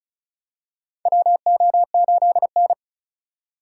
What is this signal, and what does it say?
Morse code, heard as a keyed tone, about 34 wpm, 700 Hz. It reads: WO8N
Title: USA Call Signs - 4 Characters - 6x Character Spacing - 35wpm